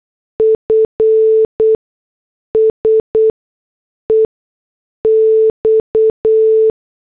Morse code, heard as a keyed tone, about 8 wpm, spaced as for 6 wpm, 440 Hz. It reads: F SEX